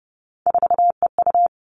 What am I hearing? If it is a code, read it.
Morse code, heard as a keyed tone, about 30 wpm, 700 Hz. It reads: 4EU
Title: Three Challenging Letter-Number Combinations 30wpm